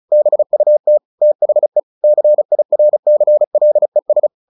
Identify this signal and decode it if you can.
Morse code, heard as a keyed tone, about 35 wpm, 600 Hz. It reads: BUT THE CIRCLES
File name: the old man and the sea 2 35wpm